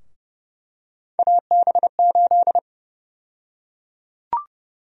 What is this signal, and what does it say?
Morse code, heard as a keyed tone, about 30 wpm, 700 Hz. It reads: AB8
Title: USA Call Sign Prefixes - Repeated 2x - 30wpm